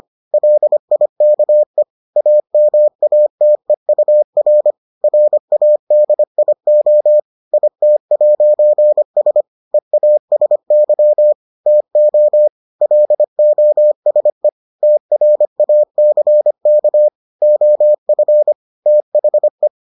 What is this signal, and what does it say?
Morse code, heard as a keyed tone, about 25 wpm, 600 Hz. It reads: LIKE AMATEUR RADIO IT'S EASY TO LOSE TRACK OF THE